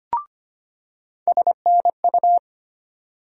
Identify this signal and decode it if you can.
Morse code, heard as a keyed tone, about 25 wpm, 700 Hz. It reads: SNU
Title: USA Call Sign Postfixes - 3 Letters 25wpm